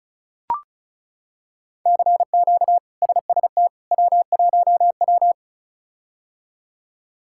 Morse code, heard as a keyed tone, about 36 wpm, 700 Hz. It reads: CQ SST W1W
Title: K1USN SST QSO Practice - 35wpm